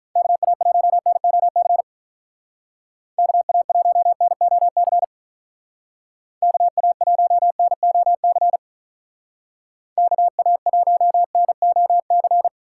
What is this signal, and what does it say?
Morse code, heard as a keyed tone, about 43 wpm, 700 Hz. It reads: KA1NOC KA1NOC KA1NOC KA1NOC
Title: Speed Racing - USA 2x3 Call Signs 35wpm